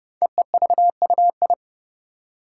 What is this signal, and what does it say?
Morse code, heard as a keyed tone, about 30 wpm, 690 Hz. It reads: EEVUI
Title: Code Groups - 5 Letters - 30wpm